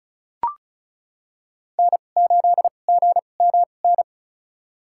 Morse code, heard as a keyed tone, about 36 wpm, spaced as for 25 wpm, 700 Hz. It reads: N8GMN
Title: USA Call Signs - 5 Characters - 2x Character Spacing - 35wpm